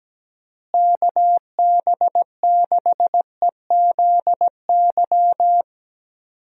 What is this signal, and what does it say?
Morse code, heard as a keyed tone, about 17 wpm, 700 Hz. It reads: KB6EZY
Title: USA Call Signs - 6 Characters - Repeated 3x - 17wpm